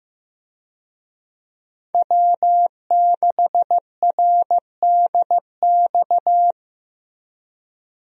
Morse code, heard as a keyed tone, about 16 wpm, 700 Hz. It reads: W6RDX
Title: USA Call Signs - 5 Characters - Repeated 2x - 15wpm